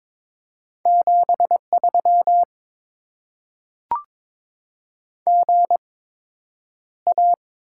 Morse code, heard as a keyed tone, about 22 wpm, spaced as for 9 wpm, 700 Hz. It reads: 73 GA
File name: Top 20 QSO Elements - 8x Character Spacing - 22wpm